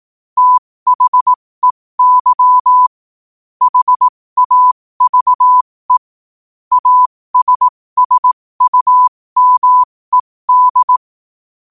Morse code, heard as a keyed tone, about 18 wpm, spaced as for 15 wpm, 1000 Hz. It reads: THEY HAVE ASSUMED